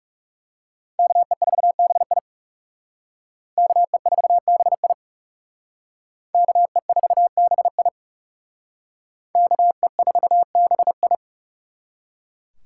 Morse code, heard as a keyed tone, about 39 wpm, 700 Hz. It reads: KE4BI KE4BI KE4BI KE4BI